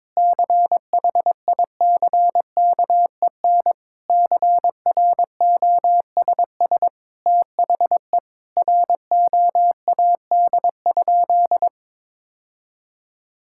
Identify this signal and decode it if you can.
Morse code, heard as a keyed tone, about 22 wpm, 700 Hz. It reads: CHICKEN CROSS THE ROAD?
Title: Sentences from Top 1000 Words - Review Part 5 - 22wpm